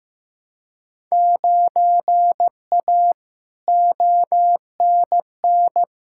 Morse code, heard as a keyed tone, about 15 wpm, 700 Hz. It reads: 9A ONN
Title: Field Day Exchanges Encore - 15wpm